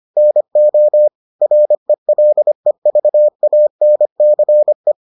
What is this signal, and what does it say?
Morse code, heard as a keyed tone, about 25 wpm, 600 Hz. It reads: NO RELEVANCE